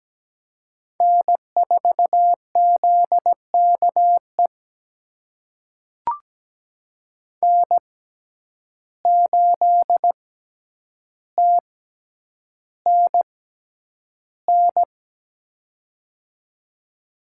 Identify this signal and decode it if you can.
Morse code, heard as a keyed tone, about 17 wpm, spaced as for 9 wpm, 700 Hz. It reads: N4ZKE N8TNN
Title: USA Call Signs - 5 Characters - 6x Character Spacing - 17wpm